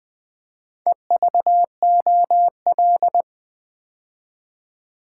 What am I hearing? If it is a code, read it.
Morse code, heard as a keyed tone, about 20 wpm, 700 Hz. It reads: EVOL
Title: Build A Word - 2 Letters At A Time - 6-Letter Words - 20wpm